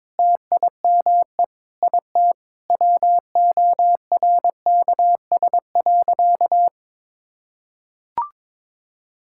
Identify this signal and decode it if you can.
Morse code, heard as a keyed tone, about 22 wpm, 700 Hz. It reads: TIME IT WORKS.
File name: practice / 22wpm / Sentences with 4-Word Phrases - 22wpm